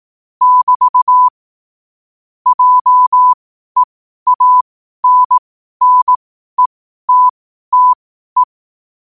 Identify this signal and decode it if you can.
Morse code, heard as a keyed tone, about 18 wpm, spaced as for 12 wpm, 1000 Hz. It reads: <BT> JEANNETTE
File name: qotdfiles / 12wpm